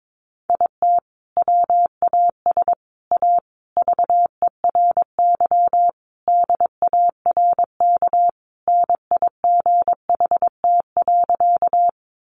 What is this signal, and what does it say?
Morse code, heard as a keyed tone, about 22 wpm, 700 Hz. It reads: IT WAS A VERY DARK NIGHT.
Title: Sentences from Top 500 Words 22wpm